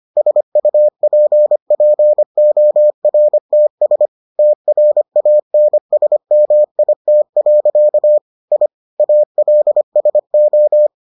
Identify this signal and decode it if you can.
Morse code, heard as a keyed tone, about 25 wpm, 600 Hz. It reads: SUPPORTS TRANSMIT. I ALSO